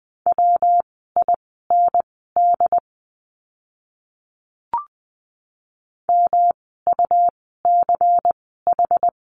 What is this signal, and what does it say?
Morse code, heard as a keyed tone, about 20 wpm, spaced as for 14 wpm, 700 Hz. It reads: WIND MUCH
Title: 115 Lesson 19f - Letter w - Words 20wpm